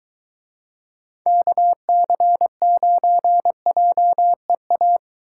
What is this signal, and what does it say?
Morse code, heard as a keyed tone, about 23 wpm, 700 Hz. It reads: KC9JEA